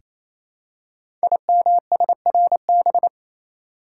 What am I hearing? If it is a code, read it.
Morse code, heard as a keyed tone, about 27 wpm, 700 Hz. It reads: IMSRB
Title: Code Groups - 5 Letters - 28wpm